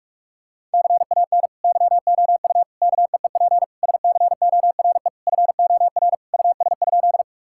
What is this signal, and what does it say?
Morse code, heard as a keyed tone, about 46 wpm, 700 Hz. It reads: CAN YOU KEEP SCORE FOR US?
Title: Sentences from Top 1600 Words 45wpm